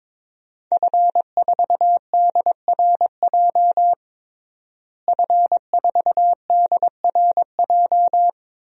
Morse code, heard as a keyed tone, about 22 wpm, 700 Hz. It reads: F4DRJ F4DRJ
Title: DXCC-100 Call Signs - 5 Characters - Repeated 3x - 22wpm